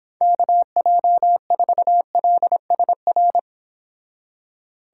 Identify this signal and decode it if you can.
Morse code, heard as a keyed tone, about 25 wpm, 700 Hz. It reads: KJ4LSR